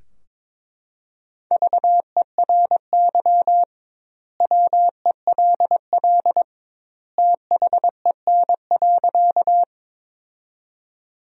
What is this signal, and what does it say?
Morse code, heard as a keyed tone, about 22 wpm, 700 Hz. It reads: VERY WELL THEN.
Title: 2x Word-Spacing - Sentences from Top 200 Words - 22wpm